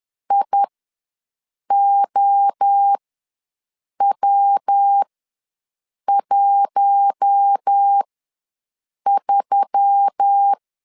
Morse code, heard as a keyed tone, about 10 wpm, spaced as for 6 wpm, 800 Hz. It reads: IOW13
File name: LEZIONI / 5 LBVC / 5 LB ws 10wpm 3616